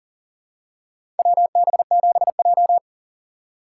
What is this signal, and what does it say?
Morse code, heard as a keyed tone, about 40 wpm, 700 Hz. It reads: WB7J
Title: USA Call Signs - 4 Characters - 6x Character Spacing - 40wpm